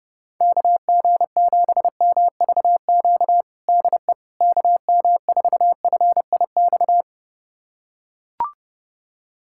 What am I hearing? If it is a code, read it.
Morse code, heard as a keyed tone, about 30 wpm, 700 Hz. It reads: KG7MVQ DE KM4FIX
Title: USA Callsign DE Callsign 30wpm